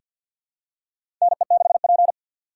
Morse code, heard as a keyed tone, about 51 wpm, 700 Hz. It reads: NE6P
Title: USA Call Signs - 3x Character Spacing - 50wpm